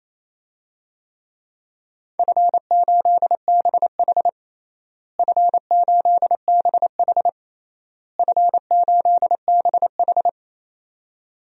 Morse code, heard as a keyed tone, about 28 wpm, 700 Hz. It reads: F8BH F8BH F8BH
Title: DXCC-100 Call Signs - 4 Characters - Repeated 3x - 28wpm